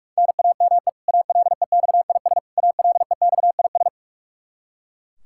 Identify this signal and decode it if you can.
Morse code, heard as a keyed tone, about 41 wpm, 700 Hz. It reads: NAME ALEXIS ALEXIS
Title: K1USN SST QSO Practice 45wpm